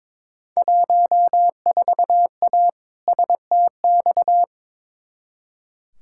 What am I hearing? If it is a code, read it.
Morse code, heard as a keyed tone, about 22 wpm, 700 Hz. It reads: 14A STX
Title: Field Day Exchanges Encore - 22wpm